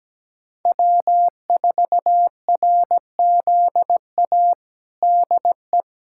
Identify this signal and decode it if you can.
Morse code, heard as a keyed tone, about 17 wpm, 700 Hz. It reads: W4RZA DE
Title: Practice QSO - Exchange 4 - Ending - 17wpm